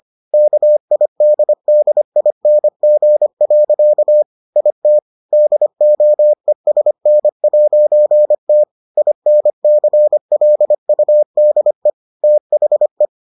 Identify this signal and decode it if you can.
Morse code, heard as a keyed tone, about 25 wpm, 600 Hz. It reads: KIDDING. IT DOESN'T INCLUDE THE